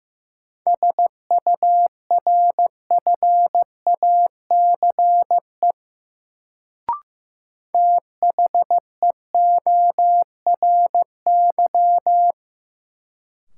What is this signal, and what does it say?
Morse code, heard as a keyed tone, about 15 wpm, 700 Hz. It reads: SURFACE THEORY
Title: Sentences from Top 600 Words - New Word Drill - 15wpm